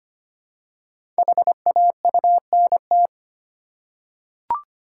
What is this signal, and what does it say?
Morse code, heard as a keyed tone, about 25 wpm, 700 Hz. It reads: HAUNT
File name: 5 Letter Words - First 3 Letters Given - 25wpm